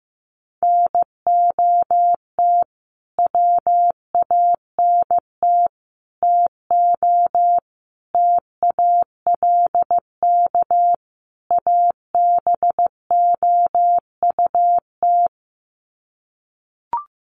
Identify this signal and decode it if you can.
Morse code, heard as a keyed tone, about 15 wpm, 700 Hz. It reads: NOT WANT TO TALK ABOUT
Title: Sets of 5 Words 15wpm